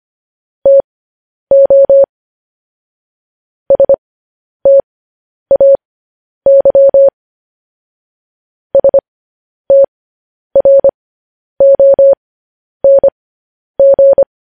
Morse code, heard as a keyed tone, about 25 wpm, spaced as for 10 wpm, 550 Hz. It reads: TO STAY STRONG
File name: podcast / News headlines-1x-25x10